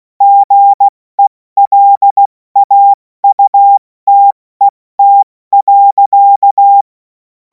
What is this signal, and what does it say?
Morse code, heard as a keyed tone, about 16 wpm, 800 Hz. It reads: GELAUTET.
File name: Morsekurs / Witze / Witz 24. März 2026 70 F80